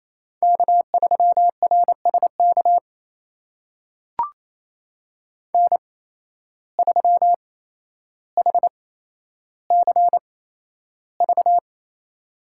Morse code, heard as a keyed tone, about 28 wpm, spaced as for 12 wpm, 700 Hz. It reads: K3RSK N3HCV
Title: USA Call Signs - 5 Characters - 8x Character Spacing - 28wpm